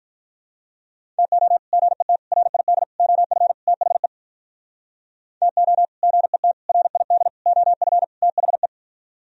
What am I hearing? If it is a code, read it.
Morse code, heard as a keyed tone, about 47 wpm, 700 Hz. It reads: TO GET RID OF THE TO GET RID OF THE